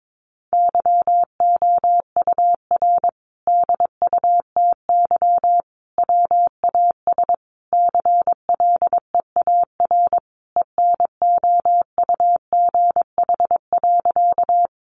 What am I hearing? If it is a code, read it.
Morse code, heard as a keyed tone, about 22 wpm, 700 Hz. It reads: YOUR DUTY WAS CLEAR ENOUGH.
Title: Sentences from Top 1500 Words 22wpm